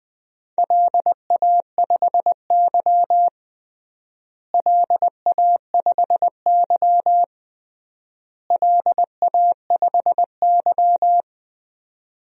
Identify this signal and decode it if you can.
Morse code, heard as a keyed tone, about 21 wpm, 700 Hz. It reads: LA5Y LA5Y LA5Y
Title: DXCC-100 Call Signs - 4 Characters - Repeated 3x - 20wpm